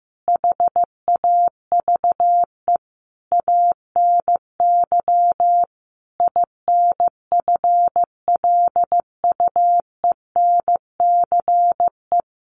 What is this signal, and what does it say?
Morse code, heard as a keyed tone, about 14 wpm, 700 Hz. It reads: HAVE ANY INFLUENCE
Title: Sentences from Top 1100 Words 15wpm